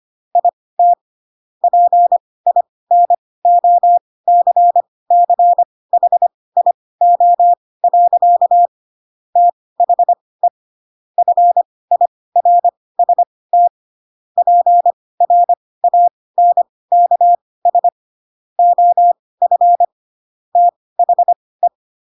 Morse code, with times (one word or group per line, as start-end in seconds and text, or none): 0.3-0.9 IT
1.6-8.7 PINOCCHIO.
9.4-10.5 THE
11.2-13.7 FIRST
14.4-17.9 PRANKS
18.6-19.8 OF
20.6-21.7 THE